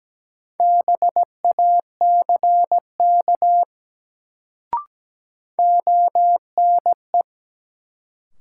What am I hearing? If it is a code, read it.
Morse code, heard as a keyed tone, about 17 wpm, 700 Hz. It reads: BACK ONE
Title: Sets of 2 Words - New Word Drill - 17wpm